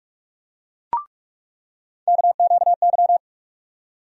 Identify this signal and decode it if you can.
Morse code, heard as a keyed tone, about 46 wpm, 700 Hz. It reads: KQY
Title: USA Call Sign Postfixes - 3 Letters 45wpm